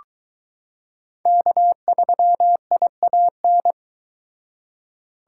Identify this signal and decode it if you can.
Morse code, heard as a keyed tone, about 22 wpm, 700 Hz. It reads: K3IAN